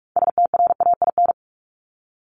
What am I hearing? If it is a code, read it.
Morse code, heard as a keyed tone, about 44 wpm, 700 Hz. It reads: STRAIN
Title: Top 1751-2000 Words 45wpm